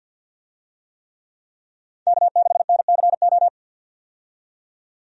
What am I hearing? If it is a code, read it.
Morse code, heard as a keyed tone, about 50 wpm, 700 Hz. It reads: K6NCO
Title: USA Call Signs - 5 Characters - 50wpm